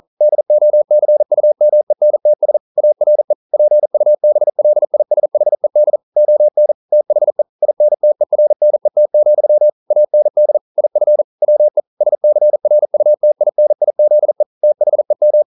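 Morse code, heard as a keyed tone, about 42 wpm, 600 Hz. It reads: DOCUMENTS ARE PUBLISHED ON THE INTERNET, AND IF WE SCRUTINIZE THEM
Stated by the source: Morse code practice at 41 wpm, from CW med Gnister.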